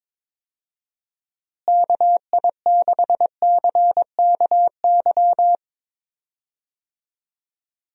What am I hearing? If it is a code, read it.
Morse code, heard as a keyed tone, about 22 wpm, 700 Hz. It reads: KI6CKY
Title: USA Call Signs - 6 Characters - Repeated 3x - 22wpm